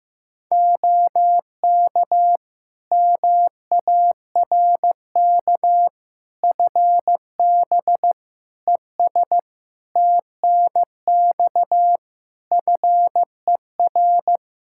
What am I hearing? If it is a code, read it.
Morse code, heard as a keyed tone, about 15 wpm, 700 Hz. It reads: OK MARK FB ES TNX FER
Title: Practice QSO - Exchange 2 - Info - 15wpm